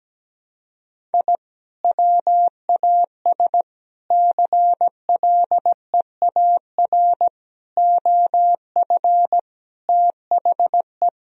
Morse code, heard as a keyed tone, about 17 wpm, 700 Hz. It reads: I WAS CLEAR OF THE
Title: Sentences from Top 1000 Words - Review Part 2 - 17wpm